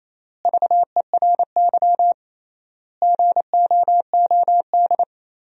Morse code, heard as a keyed tone, about 27 wpm, 700 Hz. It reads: VERY GOOD
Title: 3x Word-Spacing - Sentences from Top 200 Words - 28wpm